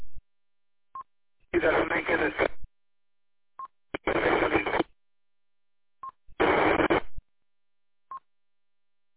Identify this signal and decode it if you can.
Speech. There is no code.